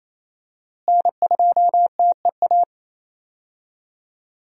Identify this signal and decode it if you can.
Morse code, heard as a keyed tone, about 28 wpm, 700 Hz. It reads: N2TEA
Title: USA Call Signs - 5 Characters - Repeated 2x - 28wpm